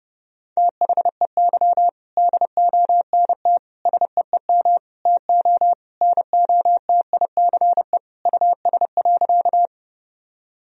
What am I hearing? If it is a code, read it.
Morse code, heard as a keyed tone, about 30 wpm, 700 Hz. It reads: THEY DONT SEEM TO NOTICE US.
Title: Sentences from Top 1000 Words - Review Part 2 - 30wpm